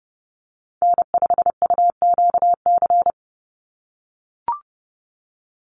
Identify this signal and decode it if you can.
Morse code, heard as a keyed tone, about 30 wpm, 700 Hz. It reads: N5UQC
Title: USA Call Signs 30wpm